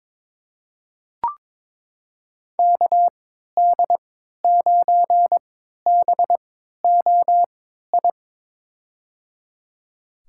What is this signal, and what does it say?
Morse code, heard as a keyed tone, about 22 wpm, spaced as for 13 wpm, 700 Hz. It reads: KD9BOI
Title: USA Call Signs - 6 Characters - 3x Character Spacing - 22wpm